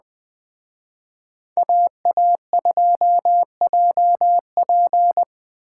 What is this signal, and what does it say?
Morse code, heard as a keyed tone, about 21 wpm, 700 Hz. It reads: AA2JP